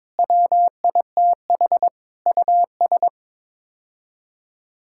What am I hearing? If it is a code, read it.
Morse code, heard as a keyed tone, about 22 wpm, 700 Hz. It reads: WITH US
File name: practice / 22wpm / Sets of 8 Words - 22wpm